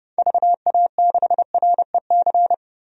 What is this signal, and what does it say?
Morse code, heard as a keyed tone, about 30 wpm, 700 Hz. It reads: VA6REC
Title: DXCC-100 Call Signs - 6 Characters - 30wpm